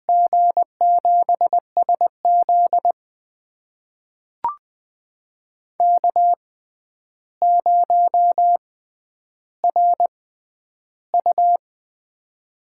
Morse code, heard as a keyed tone, about 20 wpm, spaced as for 10 wpm, 700 Hz. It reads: G7SZ K0RU
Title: USA Call Signs - 6x Character Spacing - 20wpm